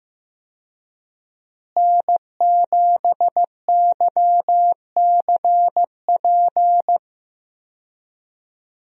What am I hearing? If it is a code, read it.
Morse code, heard as a keyed tone, about 15 wpm, 700 Hz. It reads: N7YCP
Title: USA Call Signs - 5 Characters - Repeated 2x - 15wpm